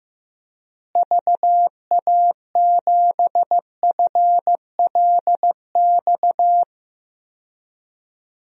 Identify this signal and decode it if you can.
Morse code, heard as a keyed tone, about 15 wpm, 700 Hz. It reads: VA7FLX
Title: DXCC-100 Call Signs - 6 Characters - Repeated 2x - 15wpm